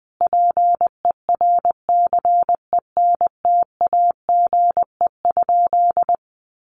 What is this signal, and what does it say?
Morse code, heard as a keyed tone, about 20 wpm, 700 Hz. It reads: PERCENTAGE?
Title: Sentences from Top 1300 Words 20wpm